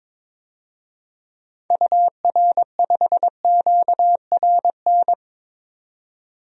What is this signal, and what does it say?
Morse code, heard as a keyed tone, about 22 wpm, 700 Hz. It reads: UR5QRN
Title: DXCC-100 Call Signs - 6 Characters - 22wpm